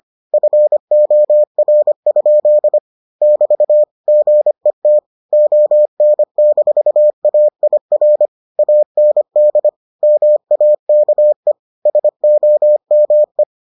Morse code, heard as a keyed tone, about 25 wpm, 600 Hz. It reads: FOR? <BT> GET ON-AIR AND MAKE SOME